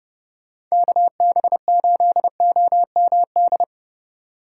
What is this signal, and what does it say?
Morse code, heard as a keyed tone, about 36 wpm, 700 Hz. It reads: KB8OMD
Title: Speed Racing - USA 2x3 Call Signs 30wpm